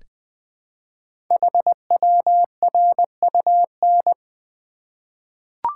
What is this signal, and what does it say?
Morse code, heard as a keyed tone, about 20 wpm, 700 Hz. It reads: HWRUN
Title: Code Groups - 5 Letters - 20wpm